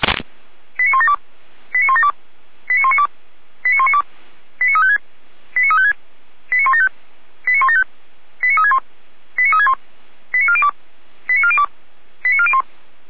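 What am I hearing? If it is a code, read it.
Five-tone selective call (ZVEI), heard as quick runs of five tones, tones of about 80 ms. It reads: 89172 89172 89192 89192 89356 89356 89175 89175 89361 89361 89402 89402 89401
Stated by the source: ZVEI five-tone selective call received with an SDR.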